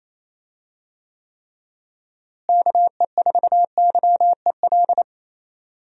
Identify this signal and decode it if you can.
Morse code, heard as a keyed tone, about 28 wpm, 700 Hz. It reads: KE4YEL
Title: USA Call Signs - 6 Characters - Repeated 3x - 28wpm